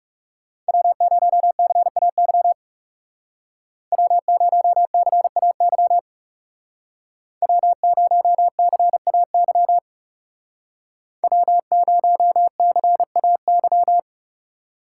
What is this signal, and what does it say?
Morse code, heard as a keyed tone, about 35 wpm, 700 Hz. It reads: W0CAY W0CAY W0CAY W0CAY
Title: Speed Racing - USA Call Signs 30wpm